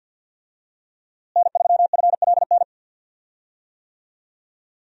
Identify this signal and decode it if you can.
Morse code, heard as a keyed tone, about 51 wpm, 700 Hz. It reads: N3FLN
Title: USA Call Signs - 5 Characters - Repeated 2x - 50wpm